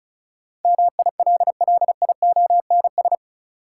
Morse code, heard as a keyed tone, about 36 wpm, 700 Hz. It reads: MILLIONS
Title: Sets of 5 Words - New Word Drill - 35wpm